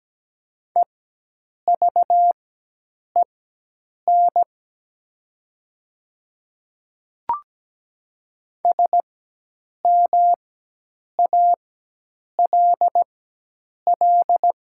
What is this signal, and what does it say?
Morse code, heard as a keyed tone, about 17 wpm, spaced as for 8 wpm, 700 Hz. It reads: EVEN SMALL
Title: 4x Character Spacing - Top 200 Words - 17wpm